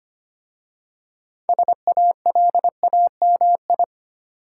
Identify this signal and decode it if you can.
Morse code, heard as a keyed tone, about 25 wpm, 690 Hz. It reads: SALAMI
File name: practice / 25wpm / Easy Words with Hints - 25wpm